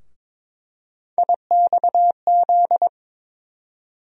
Morse code, heard as a keyed tone, about 22 wpm, 700 Hz. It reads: IXZ
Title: USA Call Sign Postfixes - 3 Letters 22wpm